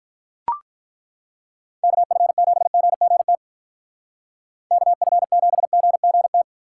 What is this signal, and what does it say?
Morse code, heard as a keyed tone, about 43 wpm, 700 Hz. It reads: KF7GGT KF7GGT
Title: Speed Racing - USA 2x3 Call Signs 35wpm